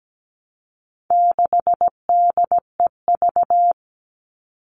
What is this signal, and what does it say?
Morse code, heard as a keyed tone, about 17 wpm, 700 Hz. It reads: 6DEV
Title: All Cut Numbers - 4-Digit 17wpm